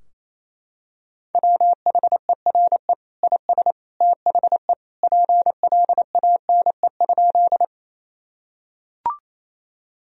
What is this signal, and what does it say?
Morse code, heard as a keyed tone, about 28 wpm, 700 Hz. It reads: WHERE IS THE PLANE?
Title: Sentences from Top 1000 Words - Review Part 3 - 28wpm